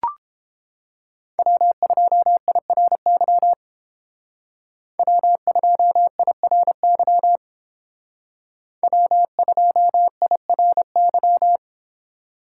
Morse code, heard as a keyed tone, about 29 wpm, 700 Hz. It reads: W2IRY W2IRY W2IRY